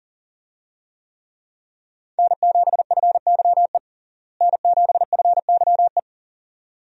Morse code, heard as a keyed tone, about 40 wpm, 700 Hz. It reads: N7FYE N7FYE
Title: USA Call Signs - 5 Characters - Repeated 2x - 40wpm